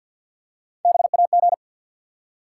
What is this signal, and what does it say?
Morse code, heard as a keyed tone, about 53 wpm, 700 Hz. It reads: BAG
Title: Three Letter Words - 50wpm